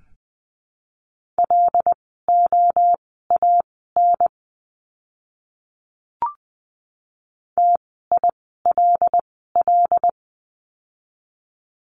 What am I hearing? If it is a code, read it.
Morse code, heard as a keyed tone, about 20 wpm, spaced as for 15 wpm, 700 Hz. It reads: LOAN TILL
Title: 075 Lesson 13f - Letter L - Words 20wpm